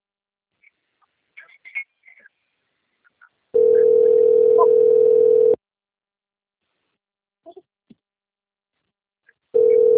Ringback tone (North America), 440 + 480 Hz.
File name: out- - - - - .71477